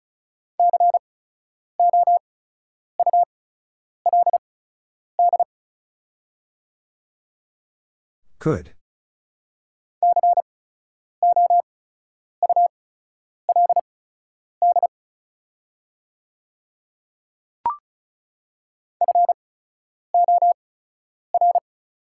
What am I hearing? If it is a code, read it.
Morse code, heard as a keyed tone, about 36 wpm, spaced as for 10 wpm, 700 Hz. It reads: COULD COULD FOR
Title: 8x Character Spacing - Top 100 Words - 35wpm